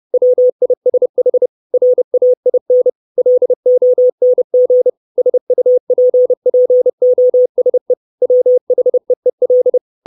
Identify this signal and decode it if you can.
Morse code, heard as a keyed tone, about 30 wpm, 500 Hz. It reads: WISH RAIN LONG SUPPOSE WHEEL
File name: 632 Words 30WPM